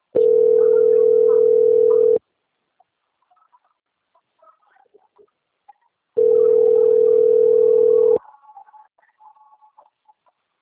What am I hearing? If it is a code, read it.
Ringback tone (North America), 440 + 480 Hz.